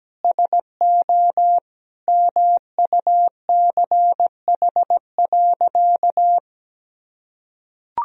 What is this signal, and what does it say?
Morse code, heard as a keyed tone, about 17 wpm, 700 Hz. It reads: SO MUCH.
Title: Sentences from Top 1000 Words - Review Part 4 - 17wpm